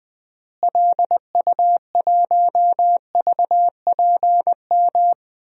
Morse code, heard as a keyed tone, about 20 wpm, 700 Hz. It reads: LU1VPM
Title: DXCC-100 Call Signs - 6 Characters - Repeated 2x - 20wpm